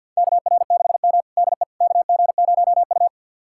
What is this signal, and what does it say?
Morse code, heard as a keyed tone, about 50 wpm, 700 Hz. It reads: KR6M DE KG0U
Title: USA Callsign DE Callsign - 4 Characters - 50wpm